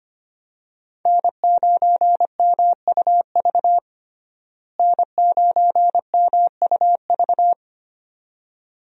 Morse code, heard as a keyed tone, about 25 wpm, 700 Hz. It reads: N9MUV N9MUV
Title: USA Call Signs - 5 Characters - Repeated 2x - 25wpm